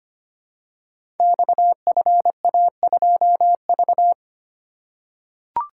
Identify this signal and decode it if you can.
Morse code, heard as a keyed tone, about 25 wpm, 700 Hz. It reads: XFA2V
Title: Code Groups - 5 Letter:Numbers - 25wpm